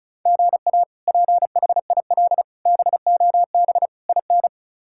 Morse code, heard as a keyed tone, about 34 wpm, 700 Hz. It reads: GA PHIL BOB IN
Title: K1USN SST QSO Practice - 35wpm